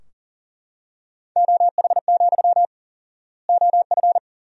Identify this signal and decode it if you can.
Morse code, heard as a keyed tone, about 40 wpm, 700 Hz. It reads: OH, OF
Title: 4x Word-Spacing - Sentences from Top 500 Words - 40wpm